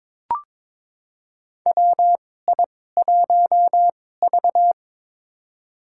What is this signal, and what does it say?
Morse code, heard as a keyed tone, about 22 wpm, spaced as for 16 wpm, 700 Hz. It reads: WI1V
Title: USA Call Signs - 4 Characters - 2x Character Spacing - 22wpm